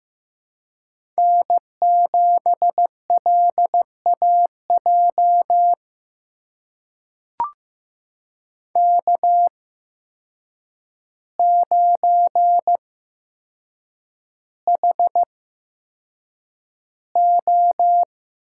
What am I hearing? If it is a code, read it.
Morse code, heard as a keyed tone, about 15 wpm, spaced as for 7 wpm, 700 Hz. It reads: N7LAJ K9HO
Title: USA Call Signs - 5 Characters - 8x Character Spacing - 15wpm